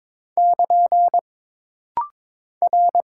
Morse code, heard as a keyed tone, about 22 wpm, 700 Hz. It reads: <KN> R
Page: This rapid-fire timing is geared to push your ICR (Instant Character Recognition) skills.